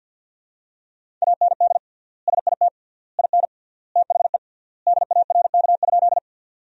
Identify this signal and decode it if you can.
Morse code, heard as a keyed tone, about 53 wpm, 700 Hz. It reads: AND SIT IN THE DARK?
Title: 3x Word-Spacing - Sentences from Top 500 Words - 50wpm